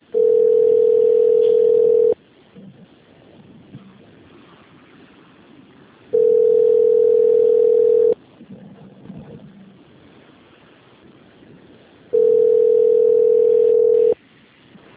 Ringback tone (North America), 440 + 480 Hz.
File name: rg-1600-unknown- - - .393